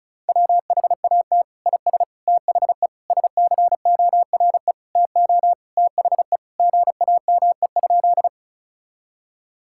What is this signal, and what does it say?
Morse code, heard as a keyed tone, about 34 wpm, 700 Hz. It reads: WHAT IS THE SCORE TO THE GAME?
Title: Sentences from Top 1600 Words 35wpm